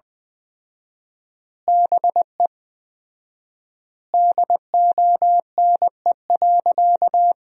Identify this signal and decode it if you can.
Morse code, heard as a keyed tone, about 20 wpm, 700 Hz. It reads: BE DONE.
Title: 4x Word-Spacing - Sentences from Top 500 Words - 20wpm